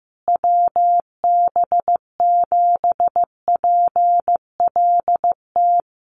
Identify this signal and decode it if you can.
Morse code, heard as a keyed tone, about 15 wpm, 700 Hz. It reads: WB7PLT
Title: USA Call Signs Repeated 3x 15wpm